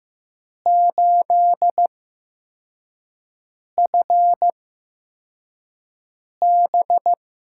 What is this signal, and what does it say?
Morse code, heard as a keyed tone, about 15 wpm, spaced as for 4 wpm, 700 Hz. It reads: 8FB
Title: USA Call Signs - 6 Characters - 8x Character Spacing - 15wpm